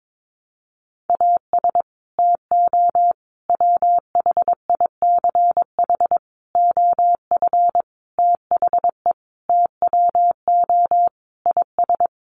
Morse code, heard as a keyed tone, about 22 wpm, 700 Hz. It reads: AS TO WHICH OF THE TWO IS
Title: Sentences from Top 1200 Words 22wpm